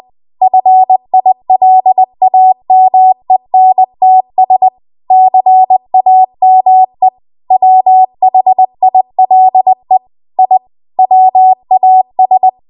Morse code, heard as a keyed tone, about 20 wpm, 750 Hz. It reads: FILAMENTS CAME WHILE I WAS
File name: Archive / 20 WPM / 20WPM